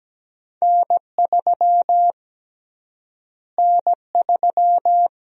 Morse code, heard as a keyed tone, about 17 wpm, 700 Hz. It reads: N3 N3
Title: USA Call Sign Prefixes - Repeated 2x - 17wpm